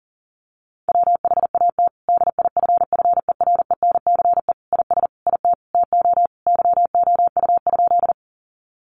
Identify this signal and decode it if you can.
Morse code, heard as a keyed tone, about 40 wpm, 700 Hz. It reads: WHAT DIFFERENCE IS IT TO YOU?
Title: Sentences from Top 800 Words 40wpm